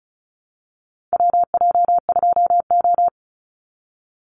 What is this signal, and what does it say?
Morse code, heard as a keyed tone, about 34 wpm, 700 Hz. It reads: WJ2O
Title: Call Signs Repeated 3x - 35wpm